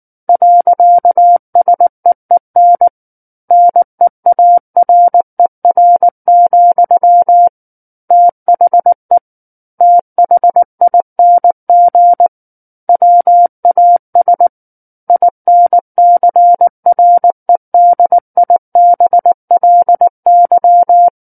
Morse code, heard as a keyed tone, about 19 wpm, 700 Hz. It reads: .SEEN NEARER, THE THING WAS INCREDIBLY
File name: Book One / 10-Chapter 10 - 19 WPM